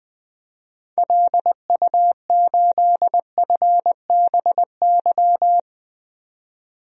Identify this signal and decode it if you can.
Morse code, heard as a keyed tone, about 21 wpm, 700 Hz. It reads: LU8FBY
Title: DXCC-100 Call Signs - 6 Characters - 20wpm